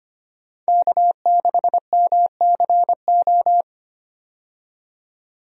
Morse code, heard as a keyed tone, about 25 wpm, 700 Hz. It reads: K6MCO
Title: USA Call Signs - 5 Characters - 25wpm